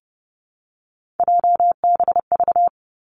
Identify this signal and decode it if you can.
Morse code, heard as a keyed tone, about 30 wpm, 700 Hz. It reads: JBV
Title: Three Challenging Letter-Number Combinations 30wpm